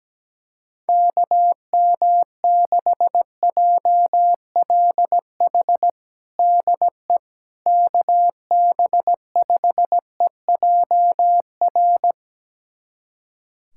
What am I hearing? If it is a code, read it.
Morse code, heard as a keyed tone, about 17 wpm, 700 Hz. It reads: KM6JLH DE KB5EJR
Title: USA Callsign DE Callsign - 6 Characters - 17wpm